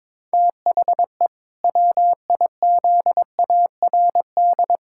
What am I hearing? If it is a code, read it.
Morse code, heard as a keyed tone, about 22 wpm, 700 Hz. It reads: THE WIZARD
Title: Pangram Challenge v2 - 22wpm